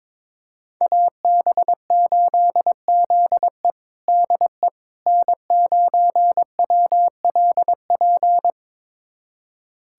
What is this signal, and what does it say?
Morse code, heard as a keyed tone, about 21 wpm, 700 Hz. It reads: AB8ZE DE N9WLP
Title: USA Callsign DE Callsign - 5 Characters - 22wpm